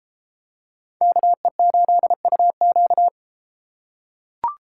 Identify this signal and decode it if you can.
Morse code, heard as a keyed tone, about 32 wpm, 700 Hz. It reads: KE8UQ